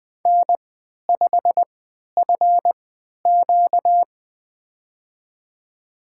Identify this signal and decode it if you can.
Morse code, heard as a keyed tone, about 20 wpm, spaced as for 11 wpm, 700 Hz. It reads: N5FQ
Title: USA Call Signs - 4 Characters - 3x Character Spacing - 20wpm